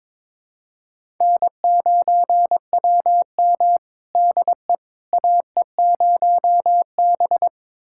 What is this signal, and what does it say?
Morse code, heard as a keyed tone, about 21 wpm, 700 Hz. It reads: N9WM DE AE0B
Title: USA Callsign DE Callsign - 4 Characters - 22wpm